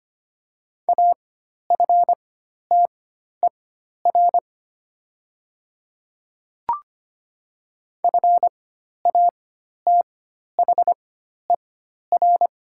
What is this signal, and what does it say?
Morse code, heard as a keyed tone, about 25 wpm, spaced as for 12 wpm, 700 Hz. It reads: AFTER FATHER
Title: 4x Character Spacing - Top 200 Words - 25wpm